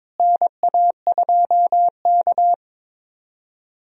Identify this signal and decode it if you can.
Morse code, heard as a keyed tone, about 22 wpm, 700 Hz. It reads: NA2K
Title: USA Call Signs - 4 Characters - Repeated 2x - 22wpm